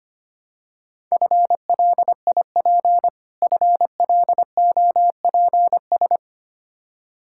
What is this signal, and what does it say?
Morse code, heard as a keyed tone, about 25 wpm, 700 Hz. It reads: FLIP FLOPS
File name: practice / 25wpm / Difficult Words with Hints - 25wpm